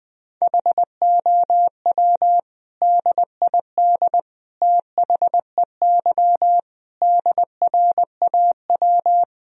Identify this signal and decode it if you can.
Morse code, heard as a keyed tone, about 21 wpm, 700 Hz. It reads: HOW DID THEY DRAW
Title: Sentences from Top 1000 Words - Review Part 1 - 20wpm